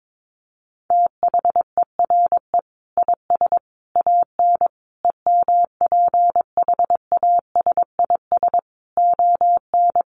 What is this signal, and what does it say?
Morse code, heard as a keyed tone, about 22 wpm, 700 Hz. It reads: THERE IS AN EMPHASIS ON
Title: Sentences from Top 1400 Words 22wpm